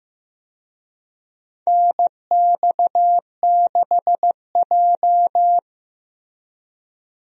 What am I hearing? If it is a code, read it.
Morse code, heard as a keyed tone, about 15 wpm, 700 Hz. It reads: NX6J
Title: USA Call Signs - 4 Characters - Repeated 3x - 15wpm